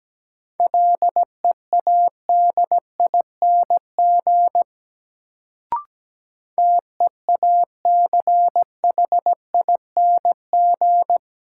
Morse code, heard as a keyed tone, about 17 wpm, 700 Hz. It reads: LEADING TEACHING
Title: Sentences from Top 1100 Words - New Word Drill - 17wpm